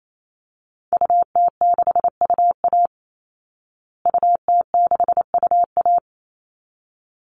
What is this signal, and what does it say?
Morse code, heard as a keyed tone, about 28 wpm, 700 Hz. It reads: UT6UA UT6UA
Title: Call Signs Repeated 3x - 28wpm